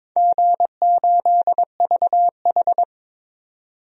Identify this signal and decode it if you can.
Morse code, heard as a keyed tone, about 22 wpm, 700 Hz. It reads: G8VH
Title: DXCC-100 Call Signs - 4 Characters - 22wpm